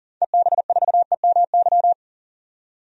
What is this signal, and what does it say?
Morse code, heard as a keyed tone, about 40 wpm, 700 Hz. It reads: EB4EMY
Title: DXCC-100 Call Signs - 6 Characters - 40wpm